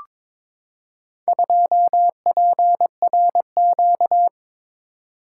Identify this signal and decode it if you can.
Morse code, heard as a keyed tone, about 22 wpm, 700 Hz. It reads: 2PRQ
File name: practice / 22wpm / Four Letter-Number Combinations - 22wpm